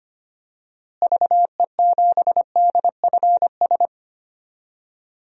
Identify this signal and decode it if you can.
Morse code, heard as a keyed tone, about 25 wpm, 700 Hz. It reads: VE7DFS
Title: DXCC-100 Call Signs - 6 Characters - Repeated 3x - 25wpm